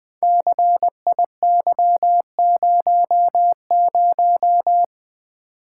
Morse code, heard as a keyed tone, about 21 wpm, 700 Hz. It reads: CIY00
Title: Code Groups - 5 Letter:Numbers - 20wpm